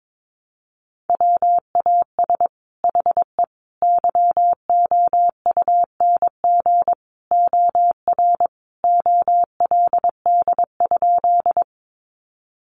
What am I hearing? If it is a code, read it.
Morse code, heard as a keyed tone, about 22 wpm, 700 Hz. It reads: WAS HE YOUNG OR OLD?
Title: Sentences from Top 400 Words 22wpm